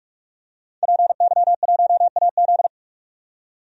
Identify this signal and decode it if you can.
Morse code, heard as a keyed tone, about 47 wpm, 700 Hz. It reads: PY1AZ
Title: DXCC-100 Call Signs - 5 Characters - 45wpm